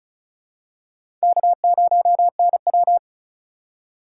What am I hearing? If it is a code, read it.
Morse code, heard as a keyed tone, about 34 wpm, 700 Hz. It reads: K0NW
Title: K1USN SST QSO Practice - 35wpm